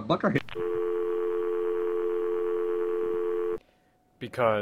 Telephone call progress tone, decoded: dial tone